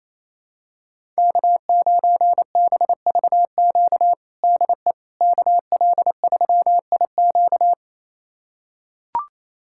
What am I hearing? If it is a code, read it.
Morse code, heard as a keyed tone, about 28 wpm, 700 Hz. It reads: K9BVQ DE KL3IQ
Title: USA Callsign DE Callsign - 5 Characters - 28wpm